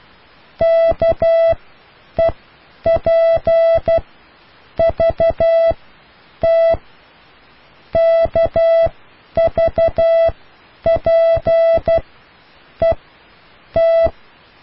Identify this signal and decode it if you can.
Morse code, heard as a keyed tone, about 12 wpm, spaced as for 8 wpm, 660 Hz. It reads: KEPVT KVPET